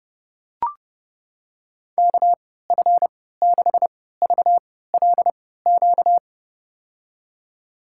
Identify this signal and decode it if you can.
Morse code, heard as a keyed tone, about 30 wpm, spaced as for 17 wpm, 700 Hz. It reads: KF6VLQ
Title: USA Call Signs - 6 Characters - 3x Character Spacing - 30wpm